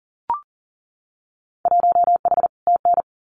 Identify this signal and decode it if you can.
Morse code, heard as a keyed tone, about 40 wpm, 700 Hz. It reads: 1H TN
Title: Winter Field Day Exchanges 40wpm